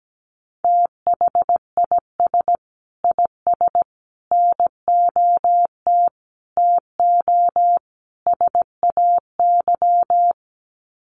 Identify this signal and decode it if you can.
Morse code, heard as a keyed tone, about 17 wpm, 700 Hz. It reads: THIS IS NOT TO SAY
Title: Sets of 5 Words Encore 17wpm